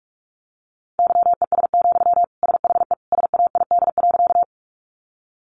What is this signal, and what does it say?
Morse code, heard as a keyed tone, about 44 wpm, 700 Hz. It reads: YES, SHE SAID.
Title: Sentences from Top 500 Words 45wpm